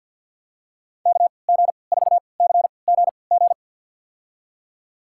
Morse code, heard as a keyed tone, about 50 wpm, spaced as for 28 wpm, 700 Hz. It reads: KG4CGG
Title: USA Call Signs - 3x Character Spacing - 50wpm